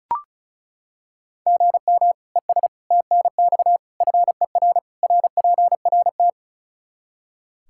Morse code, heard as a keyed tone, about 36 wpm, 700 Hz. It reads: GM ES TNX FER RPRT
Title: Practice QSO - Exchange 1 - Beginning - 35wpm